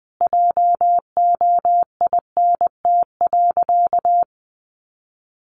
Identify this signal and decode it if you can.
Morse code, heard as a keyed tone, about 20 wpm, 700 Hz. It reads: JOINT.
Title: Sentences from Top 1300 Words 20wpm